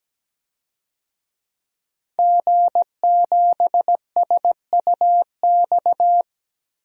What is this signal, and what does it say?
Morse code, heard as a keyed tone, about 17 wpm, 700 Hz. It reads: G7SUX
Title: DXCC-100 Call Signs - 5 Characters - Repeated 2x - 17wpm